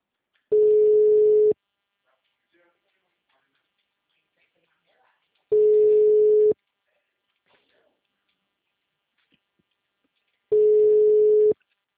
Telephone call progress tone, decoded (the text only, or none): ringback tone